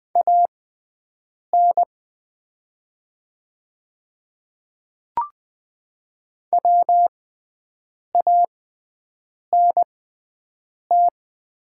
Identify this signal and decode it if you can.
Morse code, heard as a keyed tone, about 20 wpm, spaced as for 7 wpm, 700 Hz. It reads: AN WANT
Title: 6x Character Spacing - Top 200 Words - 20wpm